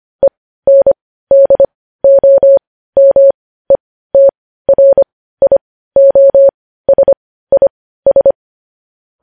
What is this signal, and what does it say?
Morse code, heard as a keyed tone, about 25 wpm, spaced as for 15 wpm, 550 Hz. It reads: ENDOMETRIOSIS
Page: news headlines in morse code at 15 wpm